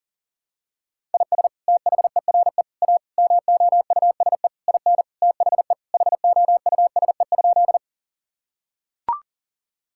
Morse code, heard as a keyed tone, about 40 wpm, 700 Hz. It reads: IS THERE A MOUSE IN THE HOUSE?
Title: Sentences from Top 1000 Words - Review Part 5 - 40wpm